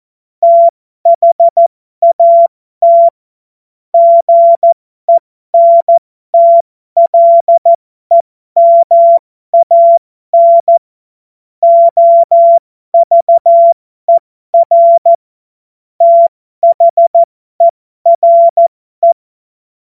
Morse code, heard as a keyed tone, about 14 wpm, spaced as for 12 wpm, 680 Hz. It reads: THAT GENTLEMAN OVER THERE
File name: Morsekurs / Witze